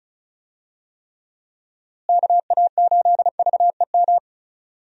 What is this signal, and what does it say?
Morse code, heard as a keyed tone, about 36 wpm, 700 Hz. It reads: KA8VEM
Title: USA Call Signs - 6 Characters - Repeated 2x - 35wpm